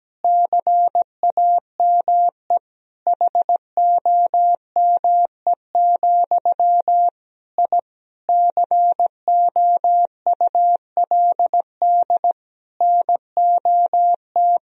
Morse code, heard as a keyed tone, about 17 wpm, 700 Hz. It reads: CAME HOME, I COULD NOT
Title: Sentences from Top 1000 Words - Review Part 3 - 17wpm